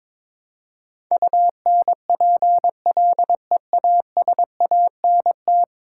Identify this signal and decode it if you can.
Morse code, heard as a keyed tone, about 21 wpm, 700 Hz. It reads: UNPLEASANT
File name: practice / 22wpm / 10 Letter Words - First 3 Letters Given - 22wpm